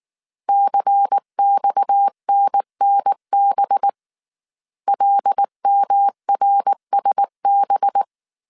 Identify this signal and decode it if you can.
Morse code, heard as a keyed tone, about 19 wpm, 800 Hz. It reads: CXNNB LMRSB
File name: CW / DET / 1 6 1500C S A 18wpm 2255